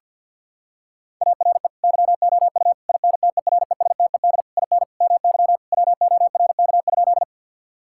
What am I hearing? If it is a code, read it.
Morse code, heard as a keyed tone, about 50 wpm, 700 Hz. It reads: ARE YOU INTERESTED IN MY WORK?
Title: Sentences from Top 1000 Words - Review Part 6 - 50wpm